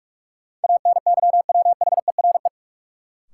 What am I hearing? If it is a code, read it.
Morse code, heard as a keyed tone, about 46 wpm, 700 Hz. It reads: ANYWHERE
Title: Sentences from Top 1400 Words - New Word Drill - 45wpm